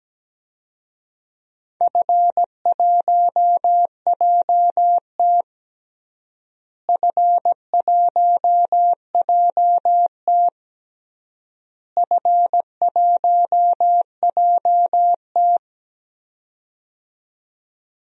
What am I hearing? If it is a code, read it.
Morse code, heard as a keyed tone, about 17 wpm, 700 Hz. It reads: F1JT F1JT F1JT
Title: DXCC-100 Call Signs - 4 Characters - Repeated 3x - 17wpm